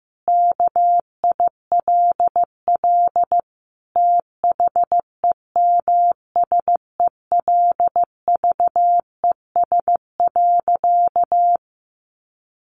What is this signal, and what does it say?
Morse code, heard as a keyed tone, about 15 wpm, 700 Hz. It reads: KILL THEMSELVES.
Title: Sentences from Top 1400 Words 15wpm